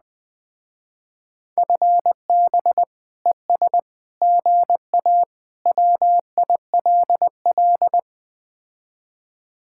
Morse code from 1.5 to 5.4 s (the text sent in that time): FB ES GA